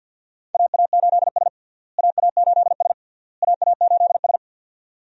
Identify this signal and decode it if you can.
Morse code, heard as a keyed tone, about 51 wpm, 700 Hz. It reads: AA8S AA8S AA8S
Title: USA Call Signs - 4 Characters - Repeated 3x - 50wpm